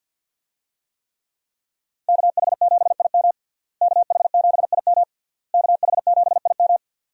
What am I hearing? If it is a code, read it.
Morse code, heard as a keyed tone, about 46 wpm, 700 Hz. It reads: KH7IM KH7IM KH7IM
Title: USA Call Signs - 5 Characters - Repeated 3x - 50wpm